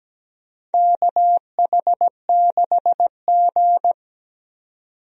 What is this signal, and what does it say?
Morse code, heard as a keyed tone, about 17 wpm, 700 Hz. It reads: KH6G
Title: DXCC-100 Call Signs - 4 Characters - 17wpm